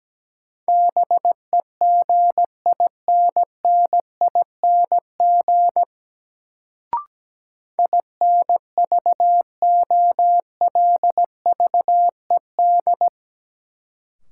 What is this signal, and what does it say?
Morse code, heard as a keyed tone, about 17 wpm, 700 Hz. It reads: BEGINNING INVOLVED
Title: Sentences from Top 800 Words - New Word Drill - 17wpm